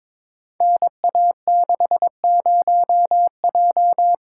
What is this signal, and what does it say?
Morse code, heard as a keyed tone, about 22 wpm, 700 Hz. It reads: NA60J
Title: Code Groups - 5 Letter:Numbers - 22wpm